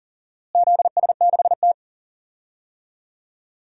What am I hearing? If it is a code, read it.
Morse code, heard as a keyed tone, about 40 wpm, 700 Hz. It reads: ZS6T
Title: DXCC-100 Call Signs - 4 Characters - Repeated 3x - 40wpm